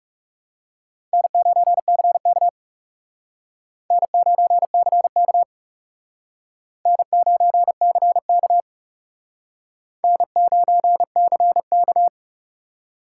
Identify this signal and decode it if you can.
Morse code, heard as a keyed tone, about 38 wpm, 700 Hz. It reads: N9CK N9CK N9CK N9CK